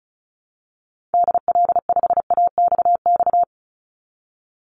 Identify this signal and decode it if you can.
Morse code, heard as a keyed tone, about 34 wpm, 700 Hz. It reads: DL5AXX
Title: Call Signs - 35wpm